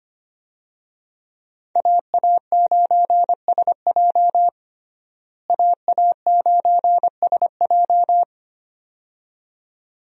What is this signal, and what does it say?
Morse code, heard as a keyed tone, about 25 wpm, 700 Hz. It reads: AA9SJ AA9SJ
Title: USA Call Signs - 5 Characters - Repeated 2x - 25wpm